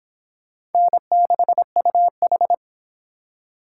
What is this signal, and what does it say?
Morse code, heard as a keyed tone, about 30 wpm, 700 Hz. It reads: N6UH